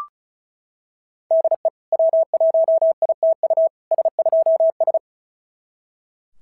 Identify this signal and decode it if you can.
Morse code, heard as a keyed tone, about 36 wpm, 690 Hz. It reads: DE W1ITU S2S
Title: SOTA QSO Practice 35wpm